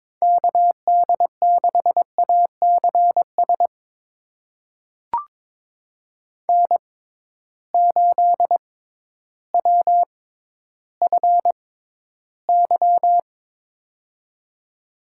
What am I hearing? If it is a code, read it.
Morse code, heard as a keyed tone, about 22 wpm, spaced as for 12 wpm, 700 Hz. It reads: KD6ACS N8WFY
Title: USA Call Signs - 6x Character Spacing - 22wpm